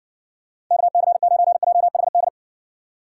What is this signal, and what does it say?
Morse code, heard as a keyed tone, about 60 wpm, 700 Hz. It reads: KC9JHD